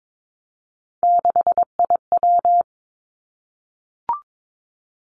Morse code, heard as a keyed tone, about 22 wpm, 700 Hz. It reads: 6IW
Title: Three Challenging Letter-Number Combinations 22wpm